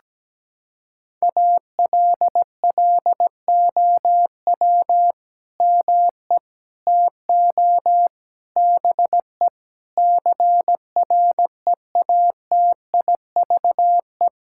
Morse code, heard as a keyed tone, about 17 wpm, 700 Hz. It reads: ALLOW ME TO BE CREATIVE
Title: Sentences from Top 1000 Words - Review Part 5 - 17wpm